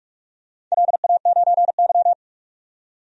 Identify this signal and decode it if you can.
Morse code, heard as a keyed tone, about 46 wpm, 700 Hz. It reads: LA9Y
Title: DXCC-100 Call Signs - 4 Characters - 45wpm